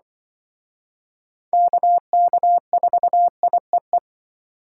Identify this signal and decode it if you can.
Morse code, heard as a keyed tone, about 25 wpm, 700 Hz. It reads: KK4IEE
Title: Speed Racing - USA 2x3 Call Signs 20wpm